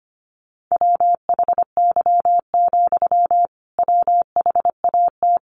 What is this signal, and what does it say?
Morse code, heard as a keyed tone, about 25 wpm, 700 Hz. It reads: WHY, WHAT
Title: Sentences from Top 200 Words 25wpm